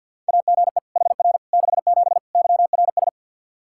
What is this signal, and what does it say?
Morse code, heard as a keyed tone, about 50 wpm, 700 Hz. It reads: AGE HR 67 YRS
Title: K1USN SST QSO Practice 50wpm